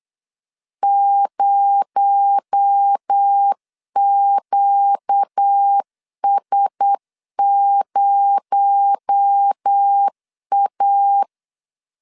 Morse code, heard as a keyed tone, about 8 wpm, 800 Hz. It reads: 0QS0A
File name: LEZIONI / 6 PKX0 / 6 PKX0 8wpm 3341